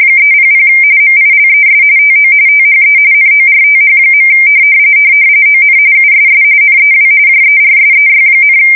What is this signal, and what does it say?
RTTY, about 45.45 baud. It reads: 300 MILES TO THE NORTH. ON USS NEW JERSEY,
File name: fsk170 / USS-New-Jersey